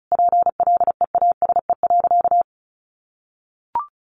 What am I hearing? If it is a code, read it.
Morse code, heard as a keyed tone, about 34 wpm, 700 Hz. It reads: PLEASE.
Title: Sentences from Top 1500 Words 35wpm